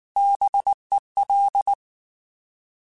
Morse code, heard as a keyed tone, about 19 wpm, 800 Hz. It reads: BEL
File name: Alphabetical / Alphabetical mp3s / Bs @18